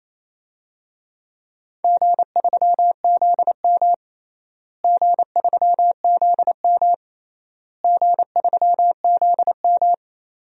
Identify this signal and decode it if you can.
Morse code, heard as a keyed tone, about 27 wpm, 700 Hz. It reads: G3ZM G3ZM G3ZM
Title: DXCC-100 Call Signs - 4 Characters - Repeated 3x - 28wpm